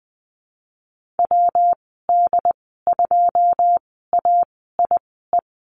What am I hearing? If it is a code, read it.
Morse code, heard as a keyed tone, about 19 wpm, spaced as for 14 wpm, 700 Hz. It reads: WD2AIE
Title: 128 Lesson 21g - Number 6 - Call Signs 20wpm